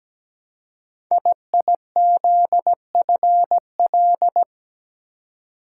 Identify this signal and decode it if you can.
Morse code, heard as a keyed tone, about 17 wpm, 700 Hz. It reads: IIZFL
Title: Code Groups - 5 Letters - 17wpm